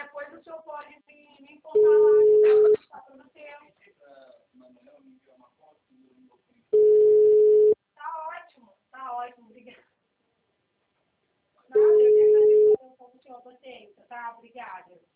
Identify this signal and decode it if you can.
Ringback tone (Europe, 425 Hz), with speech.